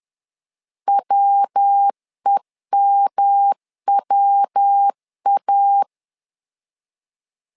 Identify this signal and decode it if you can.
Morse code, heard as a keyed tone, about 10 wpm, 800 Hz. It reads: WEMWA
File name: LEZIONI / 3 MWRO / 3 MW 10wpm 1959